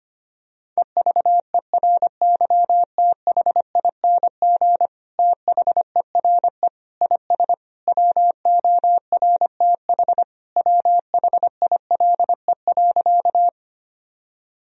Morse code, heard as a keyed tone, about 25 wpm, 700 Hz. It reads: EVERYTHING THERE IS WORTH WHILE.
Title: Sentences from Top 1000 Words - Review Part 1 - 25wpm